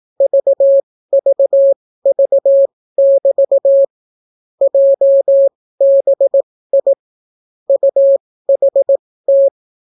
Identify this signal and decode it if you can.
Morse code, heard as a keyed tone, about 18 wpm, spaced as for 14 wpm, 550 Hz. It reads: VVV<BT> JBI UHT